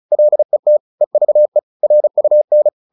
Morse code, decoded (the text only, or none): LET EVE RUN